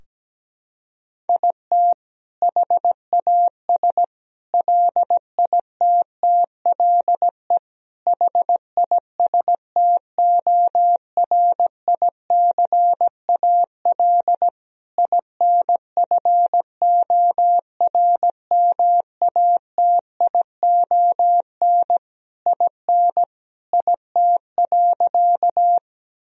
Morse code, heard as a keyed tone, about 17 wpm, 700 Hz. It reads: IT HAS LITTLE HISTORICAL INFORMATION IN IT.
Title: Sentences from Top 1000 Words - Review Part 4 - 17wpm